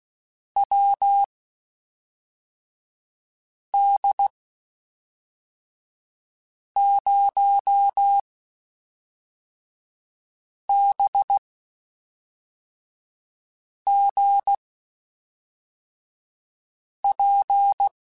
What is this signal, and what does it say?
Morse code, heard as a keyed tone, about 16 wpm, spaced as for 3 wpm, 800 Hz. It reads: WD0BGP
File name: (spaced) / QSOs Spaced-mp3s / WD0BGP 1 @5 sp